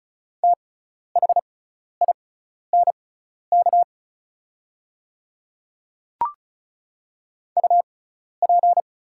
Morse code, heard as a keyed tone, about 34 wpm, spaced as for 12 wpm, 690 Hz. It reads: THINK UP
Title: 6x Character Spacing - Top 200 Words - 35wpm